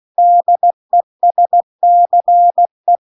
Morse code, heard as a keyed tone, about 16 wpm, 700 Hz. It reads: DESCE